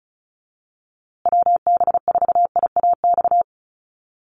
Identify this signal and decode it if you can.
Morse code, heard as a keyed tone, about 34 wpm, 700 Hz. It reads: WB4IAX
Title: USA Call Signs Repeated 3x 35wpm